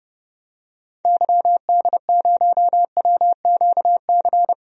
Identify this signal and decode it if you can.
Morse code, heard as a keyed tone, about 30 wpm, 700 Hz. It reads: YD0WQC
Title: DXCC-100 Call Signs - 6 Characters - 30wpm